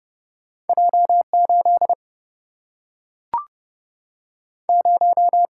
Morse code, heard as a keyed tone, about 30 wpm, 700 Hz. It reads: J8 0
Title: Single-Letter Single-Number 30wpm